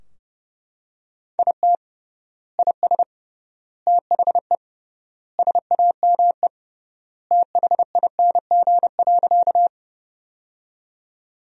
Morse code, heard as a keyed tone, about 31 wpm, 700 Hz. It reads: IT IS THE SAME THING.
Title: 3x Word-Spacing - Sentences from Top 200 Words - 30wpm